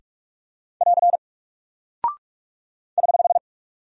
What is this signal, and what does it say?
Morse code, heard as a keyed tone, about 44 wpm, 700 Hz. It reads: <AR> <HH>
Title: Single Character (Letter:Number:Punctuation:Prosign) 45wpm